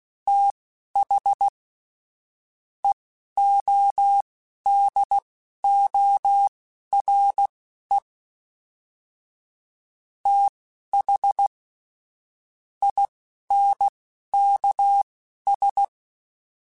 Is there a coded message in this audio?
Yes: Morse code from 0.3 to 15.8 s, heard as a keyed tone, about 16 wpm, spaced as for 11 wpm, 800 Hz. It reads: TH EODORE TH INKS